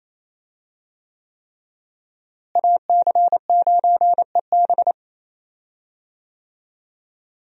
Morse code, heard as a keyed tone, about 28 wpm, 700 Hz. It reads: AC9EB
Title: USA Call Signs - 5 Characters - Repeated 3x - 28wpm